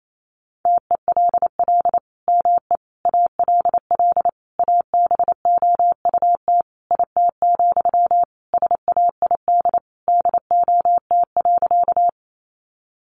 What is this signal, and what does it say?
Morse code, heard as a keyed tone, about 28 wpm, 700 Hz. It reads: TELL ME ALL ABOUT IT, SAID DOT.
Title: Sentences from Top 1500 Words 28wpm